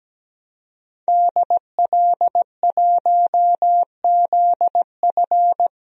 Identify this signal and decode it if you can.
Morse code, heard as a keyed tone, about 17 wpm, 700 Hz. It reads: DL1ZF
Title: DXCC-100 Call Signs - 5 Characters - 17wpm